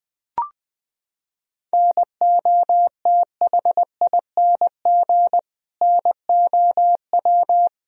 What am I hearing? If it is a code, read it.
Morse code, heard as a keyed tone, about 20 wpm, 700 Hz. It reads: NOTHING NOW
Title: Sentences from Top 1000 Words - Review Part 1 - 20wpm